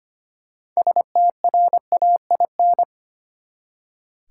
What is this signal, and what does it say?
Morse code, heard as a keyed tone, about 25 wpm, 700 Hz. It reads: STRAIN
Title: Six Letter Words - 25wpm